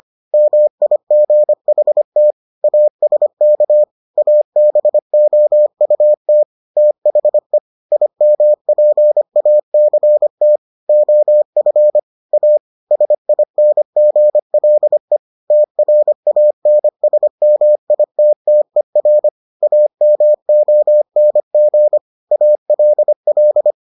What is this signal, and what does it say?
Morse code, heard as a keyed tone, about 25 wpm, 600 Hz. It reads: MIGHT ASK ABOUT THE IMPACT OF A SINGLE TRANSMITTER AMONG ALL